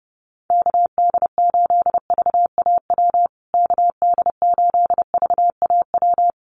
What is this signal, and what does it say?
Morse code, heard as a keyed tone, about 30 wpm, 700 Hz. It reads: KD8VAW KD8VAW
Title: USA Call Signs Repeated 3x 30wpm